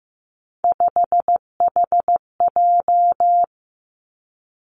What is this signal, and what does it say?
Morse code, heard as a keyed tone, about 15 wpm, 700 Hz. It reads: 5HJ
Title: Three Challenging Letter-Number Combinations 15wpm